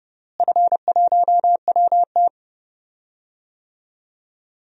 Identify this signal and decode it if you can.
Morse code, heard as a keyed tone, about 32 wpm, 700 Hz. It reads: F1WT
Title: DXCC-100 Call Signs - 4 Characters - Repeated 3x - 30wpm